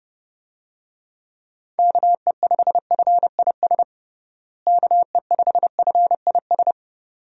Morse code, heard as a keyed tone, about 31 wpm, 700 Hz. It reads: KE5FIS KE5FIS
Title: USA Call Signs - 6 Characters - Repeated 2x - 30wpm